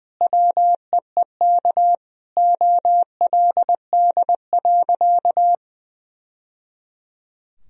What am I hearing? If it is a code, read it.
Morse code, heard as a keyed tone, about 20 wpm, 700 Hz. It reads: WEEK OLD.
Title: Sentences from Top 1000 Words - Review Part 2 - 20wpm